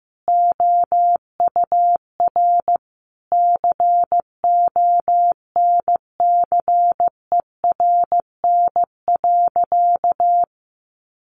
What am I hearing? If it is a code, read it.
Morse code, heard as a keyed tone, about 14 wpm, 700 Hz. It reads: OUR CONCERN.
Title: Sentences from Top 1400 Words 15wpm